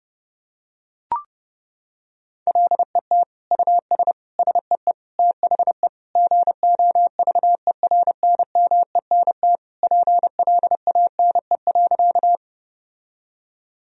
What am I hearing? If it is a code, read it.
Morse code, heard as a keyed tone, about 29 wpm, 700 Hz. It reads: LET US SEE THE GOVERNMENT PLANE.
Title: Sentences from Top 1000 Words - Review Part 3 - 30wpm